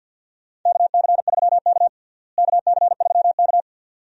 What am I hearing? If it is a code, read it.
Morse code, heard as a keyed tone, about 52 wpm, 700 Hz. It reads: KC3K KC3K
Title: USA Call Signs - 4 Characters - Repeated 2x - 50wpm